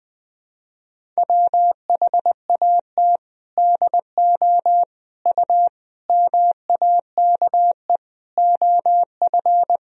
Morse code, heard as a keyed tone, about 20 wpm, 700 Hz. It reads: WHAT DO U MAKE OF
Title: Sentences from Top 100 Words and Abbreviations 20wpm